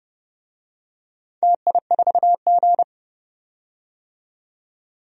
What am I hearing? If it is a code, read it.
Morse code, heard as a keyed tone, about 30 wpm, 700 Hz. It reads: TI4G
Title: DXCC-100 Call Signs - 4 Characters - 30wpm